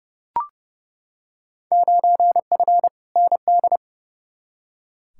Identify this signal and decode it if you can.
Morse code, heard as a keyed tone, about 30 wpm, 700 Hz. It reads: 9F ND
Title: Field Day Exchanges Encore - 30wpm